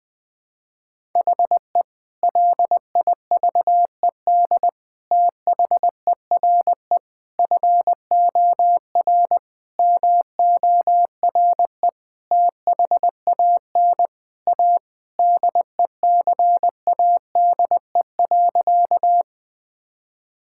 Morse code, heard as a keyed tone, about 20 wpm, 700 Hz. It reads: HE LIVED THERE FOR MORE THAN A DECADE.
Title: Sentences with 5-Word Phrases - Spoken Beginning - 20wpm